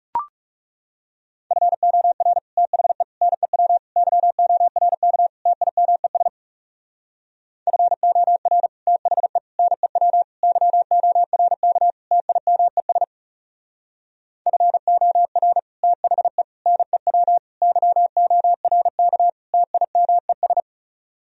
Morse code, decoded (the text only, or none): FOR THE NEW YORK TIMES FOR THE NEW YORK TIMES FOR THE NEW YORK TIMES